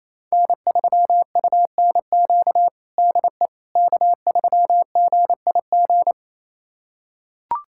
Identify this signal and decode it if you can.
Morse code, heard as a keyed tone, about 28 wpm, 690 Hz. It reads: N3UNQ DE K3GIG
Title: USA Callsign DE Callsign - 5 Characters - 28wpm